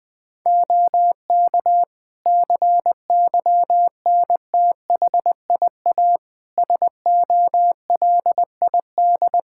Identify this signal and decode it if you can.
Morse code, heard as a keyed tone, about 20 wpm, 700 Hz. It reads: OK CYNTHIA SOLID
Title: Practice QSO - Exchange 3 - More Info - 20wpm